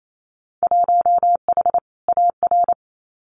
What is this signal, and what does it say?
Morse code, heard as a keyed tone, about 28 wpm, 700 Hz. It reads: 1H AR
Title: Winter Field Day Exchanges Encore 28wpm